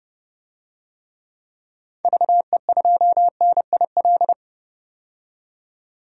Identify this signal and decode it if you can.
Morse code, heard as a keyed tone, about 30 wpm, 700 Hz. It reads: VE2NIL
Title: DXCC-100 Call Signs - 6 Characters - 30wpm